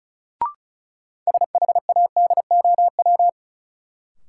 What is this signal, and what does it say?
Morse code, heard as a keyed tone, about 34 wpm, 700 Hz. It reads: SHADOW
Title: Sets of 5 Words Encore - New Word Drill - 35wpm